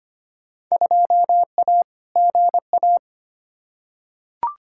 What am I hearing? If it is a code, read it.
Morse code, heard as a keyed tone, about 25 wpm, 700 Hz. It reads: 2A GA
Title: Field Day Exchanges - 25wpm